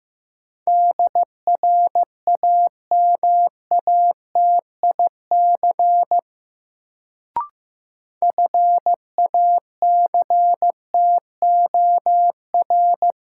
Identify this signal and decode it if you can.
Morse code, heard as a keyed tone, about 15 wpm, 700 Hz. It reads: DRAMATIC FACTOR
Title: Sentences from Top 1300 Words - New Word Drill Alt - 15wpm